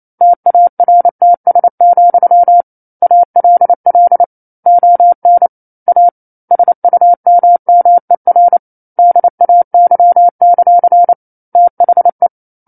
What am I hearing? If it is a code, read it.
Morse code, heard as a keyed tone, about 28 wpm, 700 Hz. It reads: TARTS, ALL ON A SUMMER DAY; THE
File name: 09-Chapter-09 28 WPM